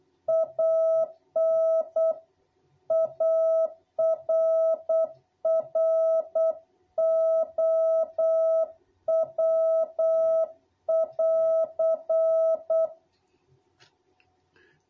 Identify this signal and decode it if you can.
Morse code, heard as a keyed tone, about 8 wpm, 660 Hz. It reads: AN ARROW<AR>
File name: jh8cbh / 2021cw